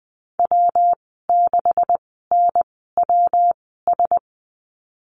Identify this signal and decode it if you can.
Morse code, heard as a keyed tone, about 20 wpm, spaced as for 15 wpm, 700 Hz. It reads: W6NWS
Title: 154 Lesson 25g - Letter p - Call Signs 20wpm